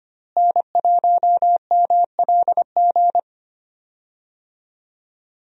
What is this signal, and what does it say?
Morse code, heard as a keyed tone, about 25 wpm, 700 Hz. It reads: N1MLG
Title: USA Call Signs - 5 Characters - Repeated 2x - 25wpm